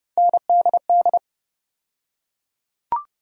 Morse code, heard as a keyed tone, about 30 wpm, 700 Hz. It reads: NDD
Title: USA Call Sign Postfixes - 3 Letters 30wpm